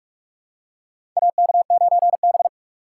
Morse code, heard as a keyed tone, about 44 wpm, 700 Hz. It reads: AK9B
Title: USA Call Signs - 4 Characters - 6x Character Spacing - 45wpm